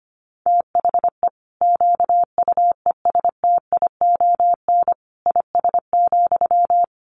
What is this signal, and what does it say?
Morse code, heard as a keyed tone, about 25 wpm, 700 Hz. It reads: THE QUESTION IS,
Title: Sentences from Top 1000 Words 25wpm